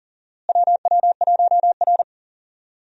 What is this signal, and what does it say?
Morse code, heard as a keyed tone, about 40 wpm, 700 Hz. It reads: WW1R
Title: USA Call Signs - 4 Characters - Repeated 2x - 40wpm